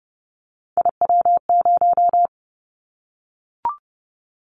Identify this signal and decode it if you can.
Morse code, heard as a keyed tone, about 30 wpm, 700 Hz. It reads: IW0
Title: Callsign Prefixes for Top 25 DXCC - Repeated 3x - 30wpm